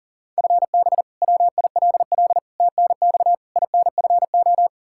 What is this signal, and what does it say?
Morse code, heard as a keyed tone, about 40 wpm, 700 Hz. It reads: FB WILL TNX INFO
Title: Practice QSO - Exchange 3 - More Info - 40wpm